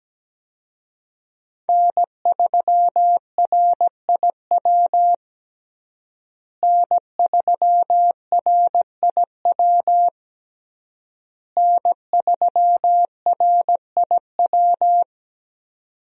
Morse code, heard as a keyed tone, about 17 wpm, 700 Hz. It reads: N3RIW N3RIW N3RIW
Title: USA Call Signs - 5 Characters - Repeated 3x - 17wpm